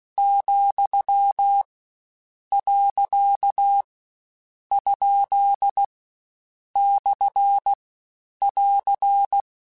Morse code, heard as a keyed tone, about 16 wpm, 800 Hz. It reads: , . ? / <AR>
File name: Short Runs / Short Runs-mp3s / Words @15